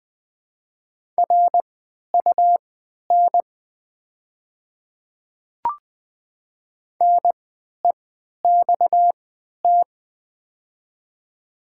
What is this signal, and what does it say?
Morse code, heard as a keyed tone, about 20 wpm, spaced as for 11 wpm, 700 Hz. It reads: RUN NEXT
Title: 3x Character Spacing - Top 300 Words - 20wpm